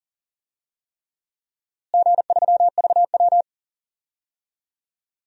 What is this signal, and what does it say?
Morse code, heard as a keyed tone, about 40 wpm, 700 Hz. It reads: G3VW
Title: DXCC-100 Call Signs - 4 Characters - Repeated 3x - 40wpm